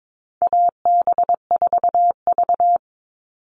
Morse code, heard as a keyed tone, about 22 wpm, 700 Hz. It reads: AB4V
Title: All Cut Numbers - 4-Digit 22wpm